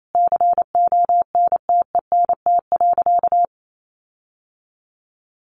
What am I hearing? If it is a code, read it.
Morse code, heard as a keyed tone, about 28 wpm, 700 Hz. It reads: CONTENT.
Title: Sentences from Top 1400 Words 28wpm